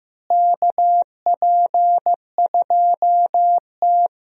Morse code, heard as a keyed tone, about 15 wpm, 700 Hz. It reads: KP2T
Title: DXCC-100 Call Signs - 4 Characters - Repeated 3x - 15wpm